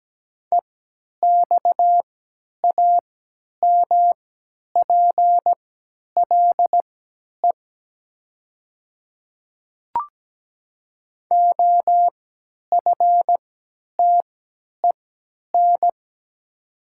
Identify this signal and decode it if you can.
Morse code, heard as a keyed tone, about 17 wpm, spaced as for 10 wpm, 690 Hz. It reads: EXAMPLE OFTEN
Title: 3x Character Spacing - Top 300 Words - 17wpm